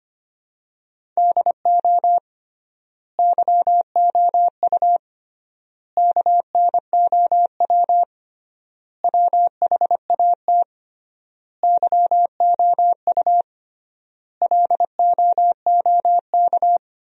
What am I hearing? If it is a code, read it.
Morse code, heard as a keyed tone, about 25 wpm, 700 Hz. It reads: DO YOU KNOW WHAT YOU LOOK
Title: 3x Word-Spacing - Sentences from Top 100 Words - 25wpm